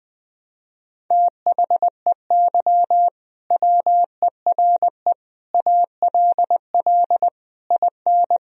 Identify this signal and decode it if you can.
Morse code, heard as a keyed tone, about 20 wpm, 700 Hz. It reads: THEY WERE ALL IN
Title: Sentences from Top 1000 Words - Review Part 3 - 20wpm